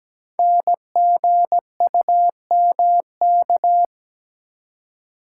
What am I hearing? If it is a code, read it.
Morse code, heard as a keyed tone, about 17 wpm, 700 Hz. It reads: NGUMK
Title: Code Groups - 5 Letters - 17wpm